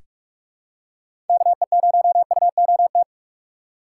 Morse code, heard as a keyed tone, about 44 wpm, 700 Hz. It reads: KE0UOT
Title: USA Call Signs - 6 Characters - 6x Character Spacing - 45wpm